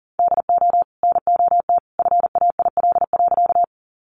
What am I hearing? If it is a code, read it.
Morse code, heard as a keyed tone, about 40 wpm, 700 Hz. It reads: DO NOT FAIL.
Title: Sentences from Top 1100 Words 40wpm